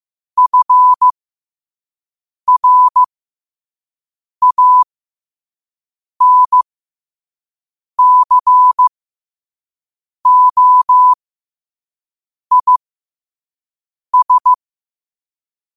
Morse code, heard as a keyed tone, about 15 wpm, spaced as for 5 wpm, 1000 Hz. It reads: FRANCOIS
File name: qotdfiles / 5wpm